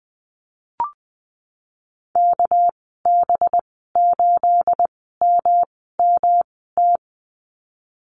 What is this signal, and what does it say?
Morse code, heard as a keyed tone, about 20 wpm, spaced as for 15 wpm, 700 Hz. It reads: KB8MMT
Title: 238 Lesson 38g - Letter q - Call Signs 20wpm